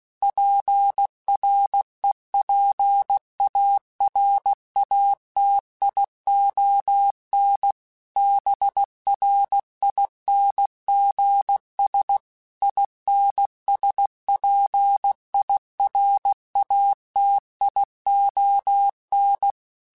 Morse code, heard as a keyed tone, about 16 wpm, 800 Hz. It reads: PREPARATION BRINGS INSPIRATION
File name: Quotes / Quotes-mp3s / Persperation @15